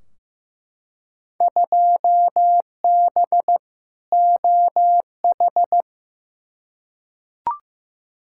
Morse code, heard as a keyed tone, about 15 wpm, 700 Hz. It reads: 2B OH
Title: Field Day Exchanges - 15wpm